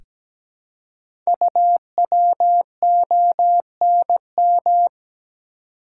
Morse code, heard as a keyed tone, about 17 wpm, 700 Hz. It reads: UWONM
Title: Code Groups - 5 Letters - 17wpm